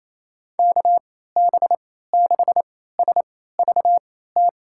Morse code, heard as a keyed tone, about 28 wpm, spaced as for 16 wpm, 700 Hz. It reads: KB6SVT
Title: USA Call Signs - 6 Characters - 3x Character Spacing - 28wpm